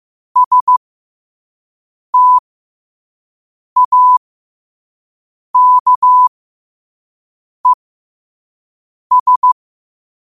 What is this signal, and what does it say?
Morse code, heard as a keyed tone, about 15 wpm, spaced as for 5 wpm, 1000 Hz. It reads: STAKES
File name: qotdfiles / 5wpm